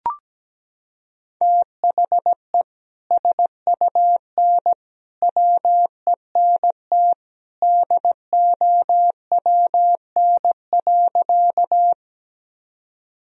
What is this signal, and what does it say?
Morse code, heard as a keyed tone, about 17 wpm, 700 Hz. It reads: THE SUN WENT DOWN.
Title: Sentences from Top 1000 Words - Review Part 1 - 17wpm